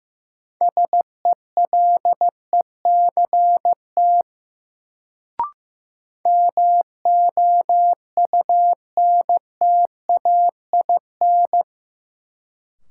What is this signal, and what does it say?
Morse code, heard as a keyed tone, about 15 wpm, 700 Hz. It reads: SELECT MOUNTAIN
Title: Sentences from Top 1300 Words - New Word Drill Alt - 15wpm